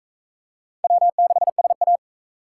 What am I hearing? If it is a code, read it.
Morse code, heard as a keyed tone, about 38 wpm, 700 Hz. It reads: W6SA